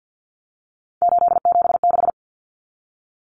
Morse code, heard as a keyed tone, about 50 wpm, 700 Hz. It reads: 876
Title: Numbers - 3-Digit 50wpm